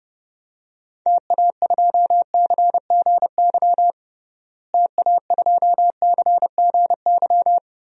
Morse code, heard as a keyed tone, about 30 wpm, 700 Hz. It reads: TA2CGY TA2CGY
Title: DXCC-100 Call Signs - 6 Characters - Repeated 2x - 30wpm